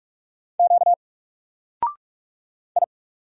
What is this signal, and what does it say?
Morse code, heard as a keyed tone, about 44 wpm, 700 Hz. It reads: Q I
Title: Single Letter-Number Alt 45wpm